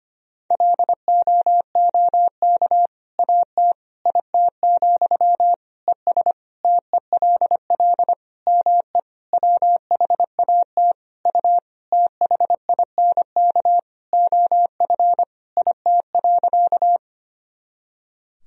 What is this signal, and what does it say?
Morse code, heard as a keyed tone, about 25 wpm, 700 Hz. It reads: LOOK AT IT, ES TELL ME WHAT U THINK OF IT.
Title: Sentences from Top 200 Words and Abbreviations 25wpm